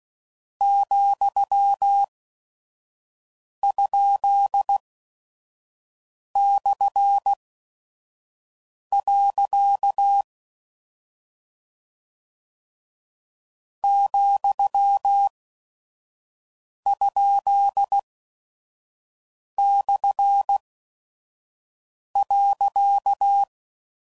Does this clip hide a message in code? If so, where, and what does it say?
Yes: Morse code from 0.6 to 23.4 s, heard as a keyed tone, about 16 wpm, spaced as for 5 wpm, 800 Hz. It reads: ,?/. ,?/.